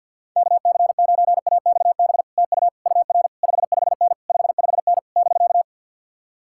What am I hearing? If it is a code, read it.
Morse code, heard as a keyed tone, about 66 wpm, spaced as for 58 wpm, 700 Hz. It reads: KC9AXB TU UR 55N 55N <BK>
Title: Speed-Racing Common QSO Phrases - 50wpm